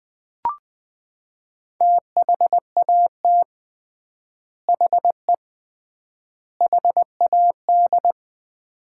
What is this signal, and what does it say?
Morse code, heard as a keyed tone, about 20 wpm, 700 Hz. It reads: THAT HE HAD
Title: 3x Word-Spacing - Sets of 3 Words - 20wpm